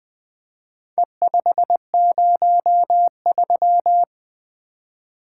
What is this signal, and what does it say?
Morse code, heard as a keyed tone, about 20 wpm, 700 Hz. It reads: E503
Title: Four Letter-Number Combinations - 20wpm